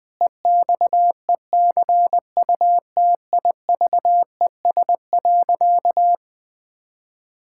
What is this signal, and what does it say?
Morse code, heard as a keyed tone, about 20 wpm, 700 Hz. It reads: EXECUTIVES.
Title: Pangram Challenge v2 - 20wpm